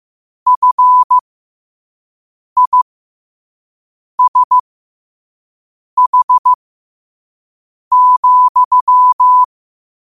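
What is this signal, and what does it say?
Morse code, heard as a keyed tone, about 15 wpm, spaced as for 5 wpm, 1000 Hz. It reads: FISH,
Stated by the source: Morse code quote of the day sent at 5 wpm.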